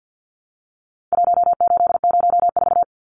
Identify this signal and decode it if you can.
Morse code, heard as a keyed tone, about 49 wpm, 700 Hz. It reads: 1804
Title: Numbers - Contest Style - 4-Digit 50wpm